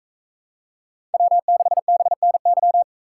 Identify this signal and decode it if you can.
Morse code, heard as a keyed tone, about 40 wpm, 700 Hz. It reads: W6BNY